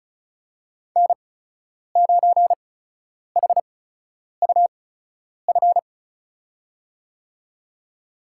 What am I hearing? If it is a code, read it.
Morse code, heard as a keyed tone, about 36 wpm, spaced as for 10 wpm, 700 Hz. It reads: N9HUF
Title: USA Call Signs - 8x Character Spacing - 35wpm